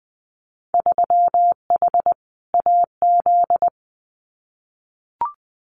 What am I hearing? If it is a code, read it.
Morse code, heard as a keyed tone, about 20 wpm, 700 Hz. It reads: 3H AZ
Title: Winter Field Day Exchanges Encore 20wpm